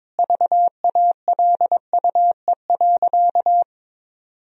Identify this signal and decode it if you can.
Morse code, heard as a keyed tone, about 22 wpm, 700 Hz. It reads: VALUE.
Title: Sentences from Top 1000 Words - Review Part 4 - 22wpm